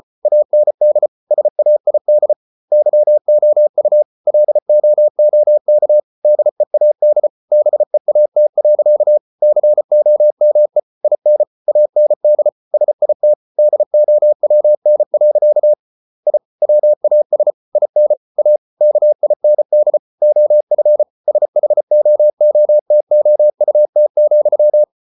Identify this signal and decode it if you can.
Morse code, heard as a keyed tone, about 34 wpm, 600 Hz. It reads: AND SAID YOU LOOK DEAD BEAT. COME IN AND SIT DOWN. I WAS IN A KIND OF SHOOTOUT,